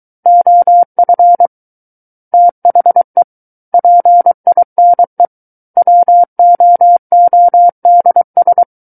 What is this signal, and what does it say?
Morse code, heard as a keyed tone, about 23 wpm, 700 Hz. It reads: OF THE PINE WOODS
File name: Book One / 14-Chapter 14 - 23 WPM